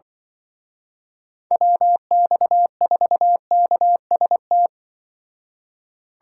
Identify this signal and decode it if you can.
Morse code, heard as a keyed tone, about 24 wpm, 700 Hz. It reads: WX4KST
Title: Speed Racing - USA Call Signs 20wpm